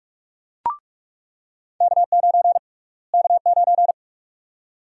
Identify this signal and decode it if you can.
Morse code, heard as a keyed tone, about 46 wpm, 690 Hz. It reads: K9 K9
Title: USA Call Sign Prefixes - Repeated 2x - 45wpm